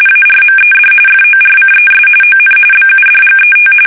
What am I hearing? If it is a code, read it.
RTTY, about 45.45 baud. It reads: ARMED FORCES ON BUDGET
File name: fsk850 / battleshipnj